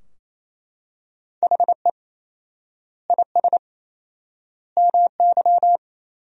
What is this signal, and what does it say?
Morse code, heard as a keyed tone, about 28 wpm, 700 Hz. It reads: HE IS MY
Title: 4x Word-Spacing - Sentences from Top 300 Words - 28wpm